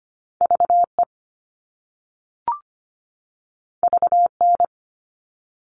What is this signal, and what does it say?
Morse code, heard as a keyed tone, about 25 wpm, 700 Hz. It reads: VE VN
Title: All Cut Numbers - 2-Digit 25wpm